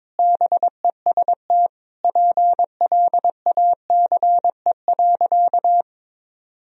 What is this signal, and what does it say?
Morse code, heard as a keyed tone, about 22 wpm, 700 Hz. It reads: BEST PLACE.
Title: Sentences from Top 1000 Words - Review Part 4 - 22wpm